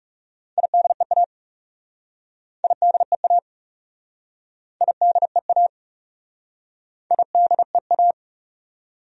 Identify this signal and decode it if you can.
Morse code, heard as a keyed tone, about 36 wpm, 700 Hz. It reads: IDEA IDEA IDEA IDEA
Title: Speed-Racing Top 500 Words 30wpm